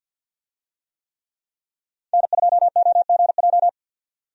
Morse code, heard as a keyed tone, about 47 wpm, 700 Hz. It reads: N2OGJ
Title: USA Call Signs - 5 Characters - 50wpm